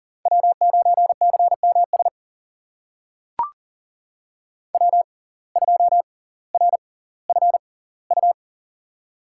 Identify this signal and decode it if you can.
Morse code, heard as a keyed tone, about 40 wpm, spaced as for 20 wpm, 700 Hz. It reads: W9CMS W2RFU
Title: USA Call Signs - 5 Characters - 6x Character Spacing - 40wpm